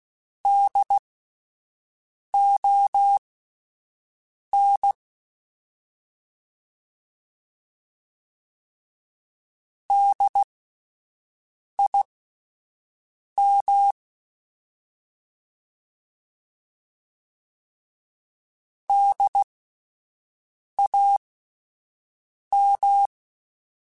Morse code, heard as a keyed tone, about 16 wpm, spaced as for 5 wpm, 800 Hz. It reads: DON DIM DAM
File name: Spaced Alphas / Spaced Alphas mp3s / Ds sp @10